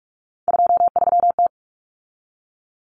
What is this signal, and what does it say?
Morse code, heard as a keyed tone, about 44 wpm, 700 Hz. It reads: 23T
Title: Common Cut Numbers - 3-Digit 45wpm